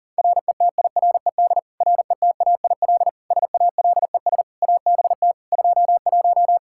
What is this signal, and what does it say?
Morse code, heard as a keyed tone, about 37 wpm, 690 Hz. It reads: RETIRED RETAIL SALES ABT 21
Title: Practice QSO - Exchange 3 - More Info - 40wpm